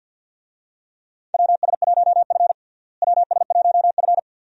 Morse code, heard as a keyed tone, about 46 wpm, 700 Hz. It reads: WS1F WS1F
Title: USA Call Signs - 4 Characters - Repeated 2x - 50wpm